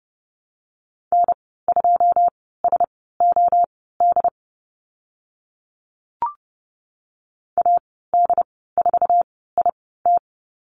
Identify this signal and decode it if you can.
Morse code, heard as a keyed tone, about 30 wpm, spaced as for 17 wpm, 700 Hz. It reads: N2SOD AD4IT
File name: Lesson 14g - Number 2 - Call Signs 30wpm